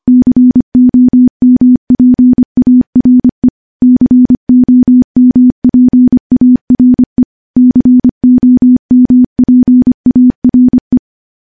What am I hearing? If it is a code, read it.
Morse code, heard as a keyed tone, about 25 wpm, 250 Hz. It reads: COMPARE COMPARE COMPARE
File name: morse / morse-training-25WPM